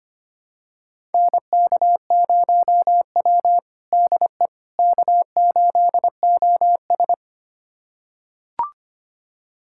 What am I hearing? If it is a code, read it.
Morse code, heard as a keyed tone, about 25 wpm, 700 Hz. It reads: NK0W DE K8OS
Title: USA Callsign DE Callsign - 4 Characters - 25wpm